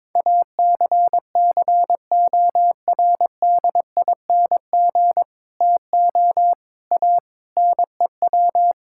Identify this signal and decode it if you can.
Morse code, heard as a keyed tone, about 22 wpm, 700 Hz. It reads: ACCORDING TO A NEW
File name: practice / 22wpm / Sets of 7 Words - 22wpm